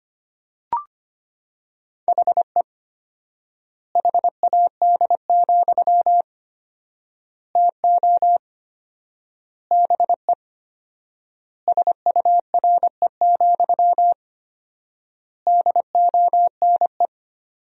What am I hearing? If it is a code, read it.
Morse code, heard as a keyed tone, about 25 wpm, 700 Hz. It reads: HE HAD, TO BE SURE, DONE
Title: 4x Word-Spacing - Sentences from Top 500 Words - 25wpm